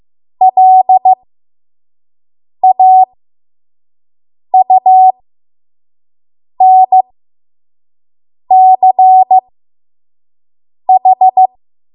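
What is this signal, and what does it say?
Morse code, heard as a keyed tone, about 15 wpm, spaced as for 5 wpm, 750 Hz. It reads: LAUNCH
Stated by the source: 5 WPM Morse code practice.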